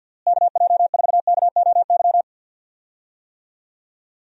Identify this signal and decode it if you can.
Morse code, heard as a keyed tone, about 52 wpm, 700 Hz. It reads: KJ4KOY
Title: USA Call Signs - 6 Characters - Repeated 3x - 50wpm